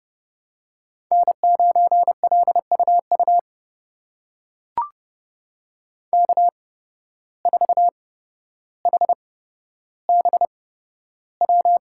Morse code, heard as a keyed tone, about 32 wpm, spaced as for 13 wpm, 700 Hz. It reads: N9LUU K4HBW
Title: USA Call Signs - 5 Characters - 8x Character Spacing - 30wpm